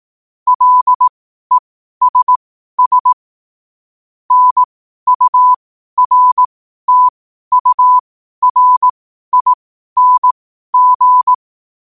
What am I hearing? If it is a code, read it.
Morse code, heard as a keyed tone, about 18 wpm, spaced as for 12 wpm, 1000 Hz. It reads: LESS NURTURING
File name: qotdfiles / 12wpm